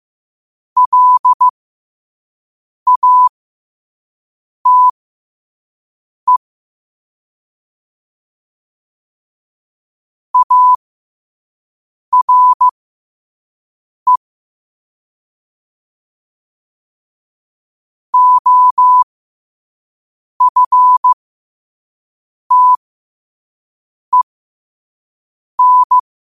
Morse code, heard as a keyed tone, about 15 wpm, spaced as for 5 wpm, 1000 Hz. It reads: LATE ARE OFTEN
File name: qotdfiles / 5wpm